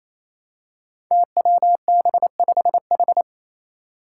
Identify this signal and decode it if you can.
Morse code, heard as a keyed tone, about 28 wpm, 700 Hz. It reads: TWB5H
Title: Five Challenging Letter-Number Combinations 28wpm